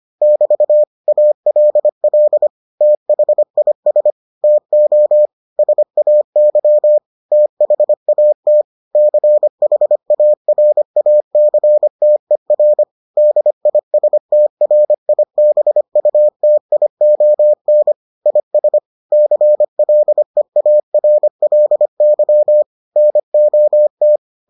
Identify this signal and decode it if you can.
Morse code, heard as a keyed tone, about 25 wpm, 600 Hz. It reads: <BT> ALL THIS TO SAY THAT CHARACTER DISTRIBUTION IS CLEARLY NOT